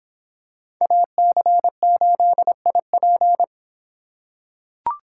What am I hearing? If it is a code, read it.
Morse code, heard as a keyed tone, about 26 wpm, 700 Hz. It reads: AC8IP